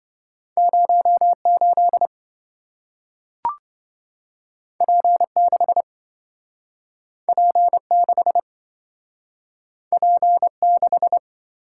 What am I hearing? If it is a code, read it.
Morse code, heard as a keyed tone, about 27 wpm, 700 Hz. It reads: 08 P6 P6 P6